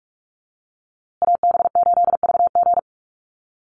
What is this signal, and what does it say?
Morse code, heard as a keyed tone, about 44 wpm, 690 Hz. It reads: AB8VG
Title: USA Call Signs Repeated 3x 45wpm